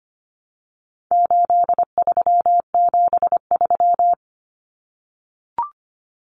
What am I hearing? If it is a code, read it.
Morse code, heard as a keyed tone, about 25 wpm, 700 Hz. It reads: 8373
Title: Numbers - 4-Digit 25wpm